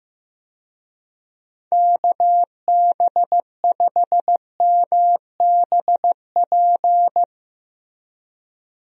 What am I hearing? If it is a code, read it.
Morse code, heard as a keyed tone, about 15 wpm, 700 Hz. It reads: KB5MBP
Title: USA Call Signs - 6 Characters - Repeated 2x - 15wpm